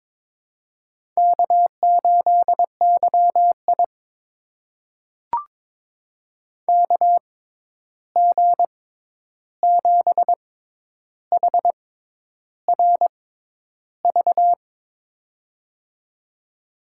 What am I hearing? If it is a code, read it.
Morse code, heard as a keyed tone, about 22 wpm, spaced as for 10 wpm, 700 Hz. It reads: K8YI KG7HRV
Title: USA Call Signs - 6x Character Spacing - 22wpm